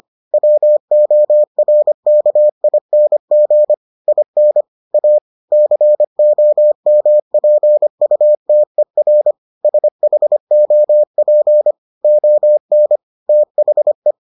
Morse code, heard as a keyed tone, about 25 wpm, 600 Hz. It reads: WORKING IN A COMPUTER SHOP ON THE